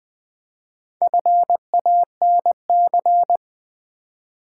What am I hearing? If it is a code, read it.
Morse code, heard as a keyed tone, about 20 wpm, 700 Hz. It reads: FANC
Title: Build A Word - 1 Letter At A Time - 5-Letter Words - 20wpm